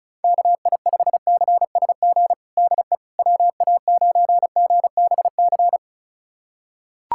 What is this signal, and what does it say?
Morse code, heard as a keyed tone, about 35 wpm, 700 Hz. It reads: KI5CSG DE WA9GBC
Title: USA Callsign DE Callsign - 6 Characters - 35wpm